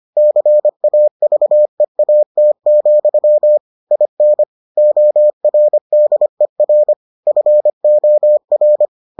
Morse code, heard as a keyed tone, about 25 wpm, 600 Hz. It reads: CAVEAT, IN ORDER FOR